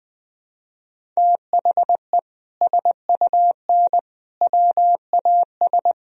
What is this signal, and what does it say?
Morse code, heard as a keyed tone, about 21 wpm, 700 Hz. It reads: THE SUN WAS
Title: Sentences from Top 1000 Words - Review Part 1 - 20wpm